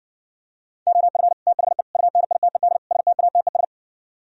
Morse code, heard as a keyed tone, about 59 wpm, 700 Hz. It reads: OF THE UNITED STATES